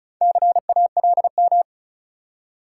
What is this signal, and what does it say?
Morse code, heard as a keyed tone, about 35 wpm, 700 Hz. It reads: CALM
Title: Four Letter Words - 35wpm